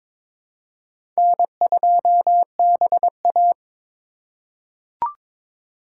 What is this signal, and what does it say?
Morse code, heard as a keyed tone, about 22 wpm, 700 Hz. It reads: N2BA
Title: USA Call Signs - 4 Characters - 3x Character Spacing - 22wpm